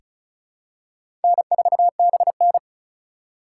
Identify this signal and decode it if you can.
Morse code, heard as a keyed tone, about 34 wpm, 700 Hz. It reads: N4BN
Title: USA Call Signs - 4 Characters - 4x Character Spacing - 35wpm